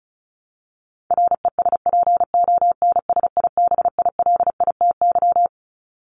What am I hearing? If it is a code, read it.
Morse code, heard as a keyed tone, about 34 wpm, 700 Hz. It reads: RESPONSIBILITY
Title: Top 1001-1250 Words 35wpm